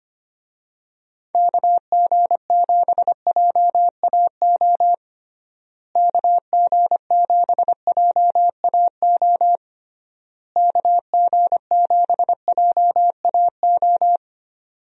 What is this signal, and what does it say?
Morse code, heard as a keyed tone, about 25 wpm, 700 Hz. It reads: KG7JAO KG7JAO KG7JAO
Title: USA Call Signs - 6 Characters - Repeated 3x - 25wpm